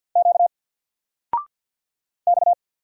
Morse code, heard as a keyed tone, about 50 wpm, 700 Hz. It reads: Q X
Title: Single Letter-Number 50wpm